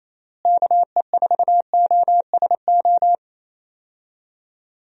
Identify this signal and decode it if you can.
Morse code, heard as a keyed tone, about 28 wpm, 700 Hz. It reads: KE4OSO
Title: USA Call Signs - 6 Characters - Repeated 3x - 28wpm